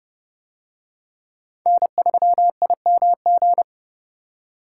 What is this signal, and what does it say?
Morse code, heard as a keyed tone, about 28 wpm, 700 Hz. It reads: N3IMG